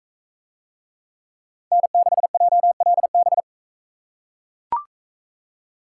Morse code, heard as a keyed tone, about 41 wpm, 700 Hz. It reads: N6JLB